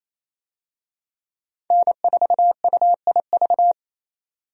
Morse code, heard as a keyed tone, about 28 wpm, 700 Hz. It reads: N4UIV
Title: USA Call Signs - 5 Characters - 28wpm